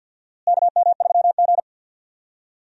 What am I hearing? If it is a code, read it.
Morse code, heard as a keyed tone, about 50 wpm, 700 Hz. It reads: KM3G
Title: USA Call Signs - 4 Characters - Repeated 2x - 50wpm